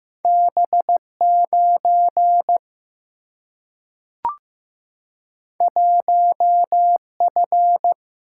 Morse code, heard as a keyed tone, about 15 wpm, 700 Hz. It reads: B9 1F
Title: Single-Letter Single-Number 15wpm